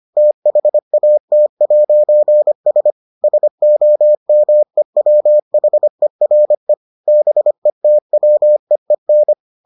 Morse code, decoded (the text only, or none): THAT'S SOMEWHERE BETWEEN